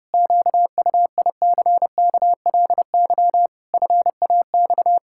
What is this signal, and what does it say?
Morse code, heard as a keyed tone, about 30 wpm, 700 Hz. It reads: QUICKLY FAX
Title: Pangram Challenge v2 - 30wpm